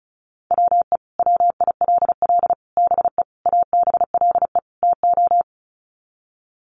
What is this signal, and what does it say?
Morse code, heard as a keyed tone, about 34 wpm, 700 Hz. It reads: WE WILL BE ABLE TO
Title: Sets of 5 Words Encore 35wpm